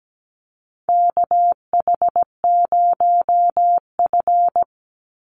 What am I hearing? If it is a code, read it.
Morse code, heard as a keyed tone, about 17 wpm, 700 Hz. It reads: KH0F
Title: USA Call Signs 17wpm